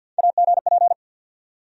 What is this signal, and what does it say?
Morse code, heard as a keyed tone, about 51 wpm, 700 Hz. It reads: AGP
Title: USA Call Sign Postfixes - 3 Letters 50wpm